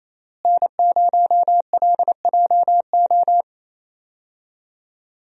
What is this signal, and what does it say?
Morse code, heard as a keyed tone, about 28 wpm, 700 Hz. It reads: N0LJO
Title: USA Call Signs - 5 Characters - Repeated 3x - 28wpm